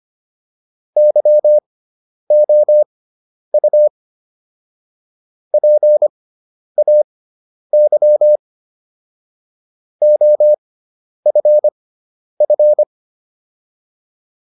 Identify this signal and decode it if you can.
Morse code, heard as a keyed tone, about 25 wpm, spaced as for 10 wpm, 600 Hz. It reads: YOU PAY OFF